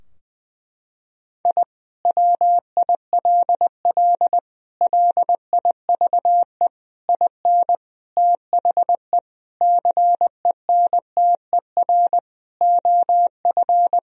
Morse code, heard as a keyed tone, about 21 wpm, 700 Hz. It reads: I WILL LIVE IN THE CENTER OF
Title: Sentences from Top 1000 Words - Review Part 2 - 20wpm